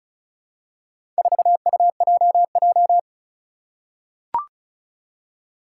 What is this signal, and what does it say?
Morse code, heard as a keyed tone, about 36 wpm, 700 Hz. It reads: 4UJJ
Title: Four Challenging Letter-Number Combinations 35wpm